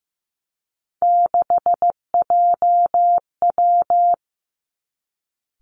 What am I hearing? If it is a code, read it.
Morse code, heard as a keyed tone, about 15 wpm, 700 Hz. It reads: 6JW
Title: Three Challenging Letter-Number Combinations 15wpm